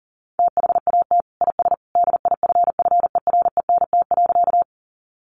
Morse code, heard as a keyed tone, about 40 wpm, 700 Hz. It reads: THAT IS DIFFERENT.
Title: Sentences from Top 600 Words 40wpm